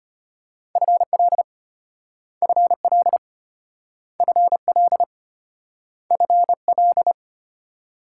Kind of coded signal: Morse code

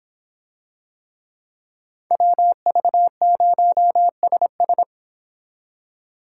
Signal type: Morse code

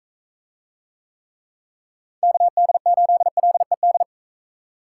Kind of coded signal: Morse code

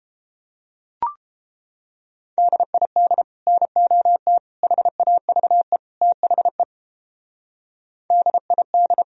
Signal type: Morse code

